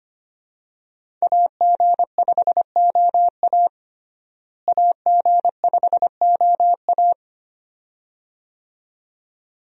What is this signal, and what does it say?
Morse code, heard as a keyed tone, about 25 wpm, 700 Hz. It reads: AG5OA AG5OA
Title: USA Call Signs - 5 Characters - Repeated 2x - 25wpm